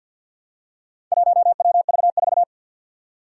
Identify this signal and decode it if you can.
Morse code, heard as a keyed tone, about 50 wpm, 700 Hz. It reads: 1WV4
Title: Four Challenging Letter-Number Combinations 50wpm